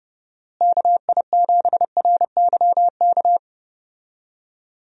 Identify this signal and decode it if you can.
Morse code, heard as a keyed tone, about 30 wpm, 700 Hz. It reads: KI7RYK
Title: USA Call Signs - 6 Characters - Repeated 2x - 30wpm